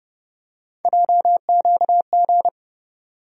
Morse code, heard as a keyed tone, about 30 wpm, 700 Hz. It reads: JQG
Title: USA Call Sign Postfixes - 3 Letters 30wpm